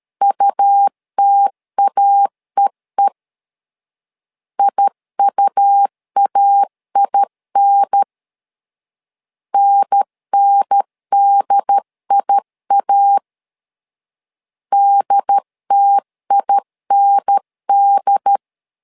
Morse code, heard as a keyed tone, about 13 wpm, 800 Hz. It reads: UTAEE IUAIN NNDIA DTIND
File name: LEZIONI / 2 UHDN / 2 UHDN 12wpm 1456